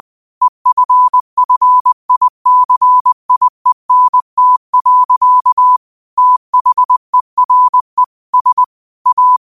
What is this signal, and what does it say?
Morse code, heard as a keyed tone, about 20 wpm, 1000 Hz. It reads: EFFICIENT. THERE S A
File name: qotdfiles / 20wpm